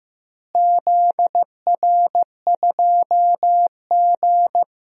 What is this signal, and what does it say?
Morse code, heard as a keyed tone, about 15 wpm, 700 Hz. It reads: ZR2G
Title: DXCC-100 Call Signs - 4 Characters - Repeated 3x - 15wpm